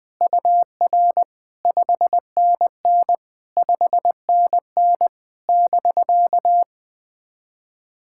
Morse code, heard as a keyed tone, about 20 wpm, 700 Hz. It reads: UR 5NN 5NN <BK>
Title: POTA QSO Practice - 20wpm